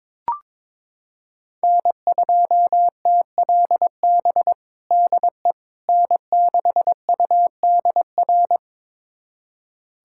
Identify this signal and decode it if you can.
Morse code, heard as a keyed tone, about 22 wpm, 700 Hz. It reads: N2TLB DE N6UDR
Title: USA Callsign DE Callsign - 5 Characters - 22wpm